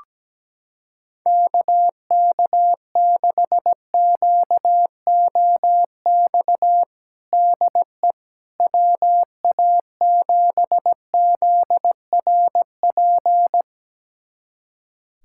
Morse code, heard as a keyed tone, about 17 wpm, 700 Hz. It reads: KK6QOX DE WA7ZRP
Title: USA Callsign DE Callsign - 6 Characters - 17wpm